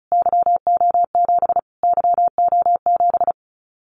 Morse code, heard as a keyed tone, about 34 wpm, 700 Hz. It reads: YO7 YO7
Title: Callsign Prefixes for Top 25 DXCC - Repeated 3x - 35wpm